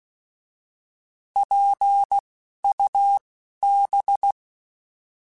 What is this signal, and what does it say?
Morse code, heard as a keyed tone, about 16 wpm, spaced as for 11 wpm, 800 Hz. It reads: PUB
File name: Alphabetical / Alphabetical mp3s / Ps @10